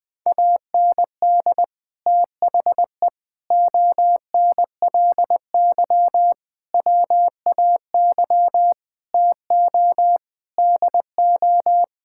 Morse code, heard as a keyed tone, about 21 wpm, 700 Hz. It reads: AND THE ONLY WAY TO DO
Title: Sentences with 9-Word Phrases - 20wpm